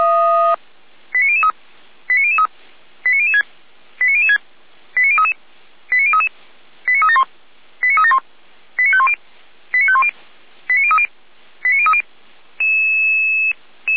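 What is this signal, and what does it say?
Five-tone selective call (ZVEI), heard as quick runs of five tones, tones of about 80 ms. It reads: 89003 89003 89006 89006 89033 89033 89371 89371 89510 89510 89030 89030
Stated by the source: ZVEI five-tone selective call received on SDR, archived by Feuerwehr Orlamünde.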